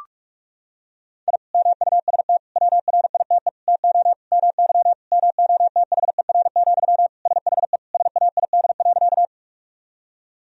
Morse code, heard as a keyed tone, about 41 wpm, 700 Hz. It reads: I MUST WRITE TO MY MOTHER, SHE SAID.
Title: Sentences from Top 1000 Words - Review Part 1 - 45wpm